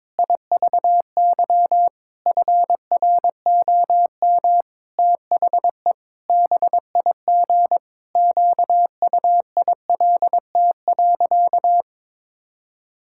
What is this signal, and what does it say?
Morse code, heard as a keyed tone, about 22 wpm, 700 Hz. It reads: IVY FROM THE BIG QUILT.
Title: Pangram Challenge v2 - 22wpm